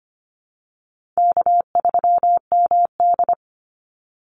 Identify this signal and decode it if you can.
Morse code, heard as a keyed tone, about 25 wpm, 700 Hz. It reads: K3MD
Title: Easy Call Signs - 25wpm